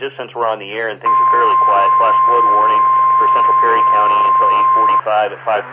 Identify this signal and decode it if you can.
Emergency Alert System (SAME) tones (the 1050 Hz attention tone), with speech. It is heard from 1.0 to 5.1 s.